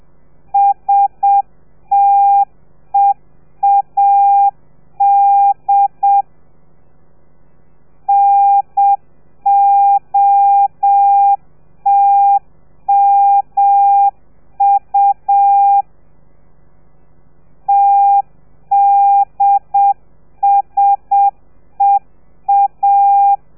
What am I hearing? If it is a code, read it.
Morse code, heard as a keyed tone, about 7 wpm, 800 Hz. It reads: STEAD NOTMU TDSEA